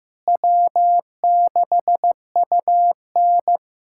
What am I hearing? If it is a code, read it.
Morse code, heard as a keyed tone, about 15 wpm, 700 Hz. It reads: W6UN
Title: USA Callsign DE Callsign - 4 Characters - 15wpm